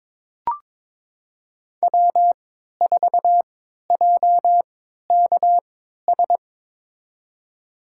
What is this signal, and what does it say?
Morse code, heard as a keyed tone, about 22 wpm, spaced as for 13 wpm, 700 Hz. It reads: W4JKS
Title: USA Call Signs - 3x Character Spacing - 22wpm